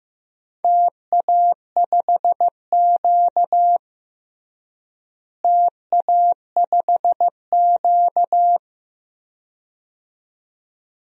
Morse code, heard as a keyed tone, about 15 wpm, 700 Hz. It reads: TA5Q TA5Q
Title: DXCC-100 Call Signs - 4 Characters - Repeated 2x - 15wpm